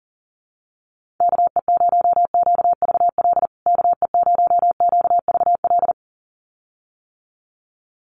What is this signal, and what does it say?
Morse code, heard as a keyed tone, about 40 wpm, 700 Hz. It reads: KE0QVL KE0QVL
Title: USA Call Signs Repeated 2x 40wpm